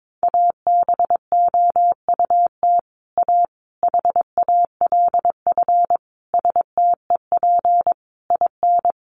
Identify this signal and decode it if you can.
Morse code, heard as a keyed tone, about 22 wpm, 700 Hz. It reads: ABOUT A HALF STEP IN
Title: Sentences from Top 1400 Words 22wpm